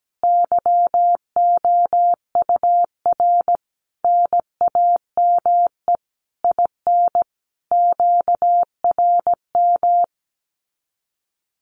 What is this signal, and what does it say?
Morse code, heard as a keyed tone, about 17 wpm, 700 Hz. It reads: YOUR NAME IN QRM
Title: Common QSO Phrases 17wpm